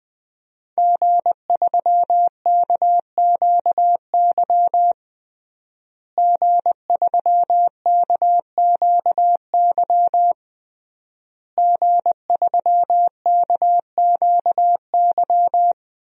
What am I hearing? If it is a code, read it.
Morse code, heard as a keyed tone, about 21 wpm, 700 Hz. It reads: G3KQY G3KQY G3KQY
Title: DXCC-100 Call Signs - 5 Characters - Repeated 3x - 20wpm